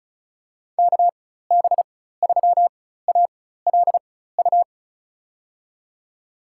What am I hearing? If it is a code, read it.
Morse code, heard as a keyed tone, about 34 wpm, spaced as for 16 wpm, 700 Hz. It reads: KB3ALU
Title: USA Call Signs - 6 Characters - 4x Character Spacing - 35wpm